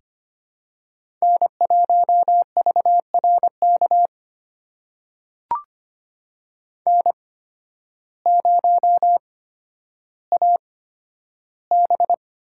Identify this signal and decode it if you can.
Morse code, heard as a keyed tone, about 25 wpm, spaced as for 12 wpm, 700 Hz. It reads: N1VRK N0AB
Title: USA Call Signs - 5 Characters - 8x Character Spacing - 25wpm